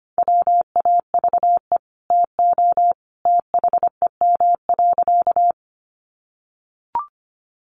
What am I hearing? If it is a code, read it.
Morse code, heard as a keyed tone, about 25 wpm, 700 Hz. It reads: WAVE TO THEM.
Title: Sentences from Top 1400 Words 25wpm